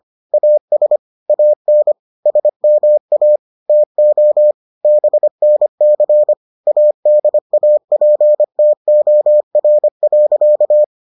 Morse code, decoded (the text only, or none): AS AN SMA TO BNC ADAPTOR.